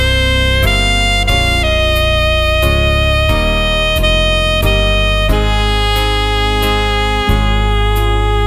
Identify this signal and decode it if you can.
Dial-up modem handshake.